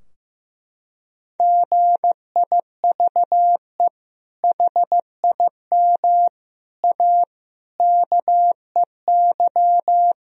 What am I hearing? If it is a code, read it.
Morse code, heard as a keyed tone, about 15 wpm, 700 Hz. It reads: GIVE HIM A KEY
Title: Sentences from Top 1000 Words - Review Part 4 - 15wpm